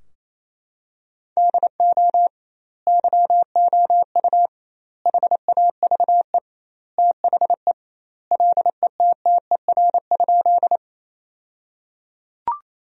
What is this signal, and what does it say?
Morse code, heard as a keyed tone, about 28 wpm, 700 Hz. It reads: DO YOU HAVE THE LETTER?
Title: 2x Word-Spacing - Sentences from Top 300 Words - 28wpm